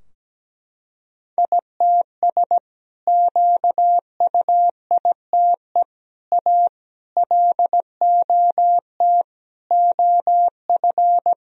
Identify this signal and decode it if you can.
Morse code, heard as a keyed tone, about 17 wpm, 700 Hz. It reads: ITS QUITE A LOT OF
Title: Sentences from Top 1000 Words - Review Part 3 - 17wpm